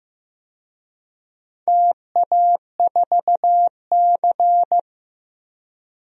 Morse code, heard as a keyed tone, about 15 wpm, 700 Hz. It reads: TA4C
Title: DXCC-100 Call Signs - 4 Characters - 15wpm